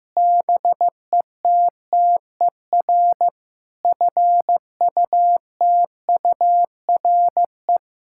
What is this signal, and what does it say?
Morse code, heard as a keyed tone, about 15 wpm, 700 Hz. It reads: BETTER FUTURE
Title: Sentences from Top 1000 Words - Review Part 6 - 15wpm